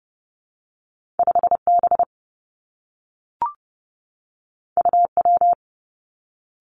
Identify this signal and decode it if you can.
Morse code, heard as a keyed tone, about 30 wpm, 700 Hz. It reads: 5B UW
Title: Two Challenging Letter-Number Combinations 30wpm